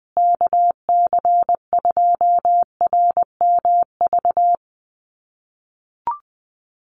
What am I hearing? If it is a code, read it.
Morse code, heard as a keyed tone, about 20 wpm, 700 Hz. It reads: KC2RMV
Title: USA Call Signs Repeated 3x 20wpm